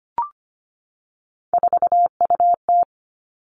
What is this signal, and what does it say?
Morse code, heard as a keyed tone, about 25 wpm, 700 Hz. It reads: 4UT
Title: Three Challenging Letter-Number Combinations 25wpm